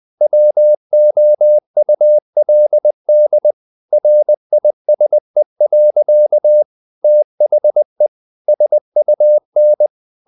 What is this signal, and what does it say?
Morse code, heard as a keyed tone, about 20 wpm, 600 Hz. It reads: WOULD RISE. THE SUN
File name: the old man and the sea / the old man and the sea part 1 20wpm